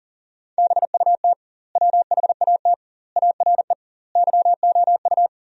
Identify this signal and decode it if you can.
Morse code, heard as a keyed tone, about 40 wpm, 700 Hz. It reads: BUT WHAT ARE YOU
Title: 2x Word-Spacing - Sentences from Top 100 Words - 40wpm